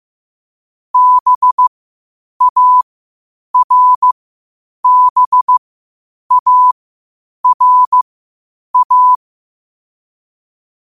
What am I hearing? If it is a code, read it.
Morse code, heard as a keyed tone, about 15 wpm, spaced as for 8 wpm, 1000 Hz. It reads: BARBARA
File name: qotdfiles / 8wpm